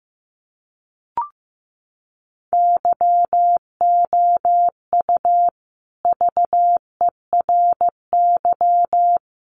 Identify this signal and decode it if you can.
Morse code, heard as a keyed tone, about 14 wpm, 700 Hz. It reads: YOU VERY
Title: Sets of 3 Words 15wpm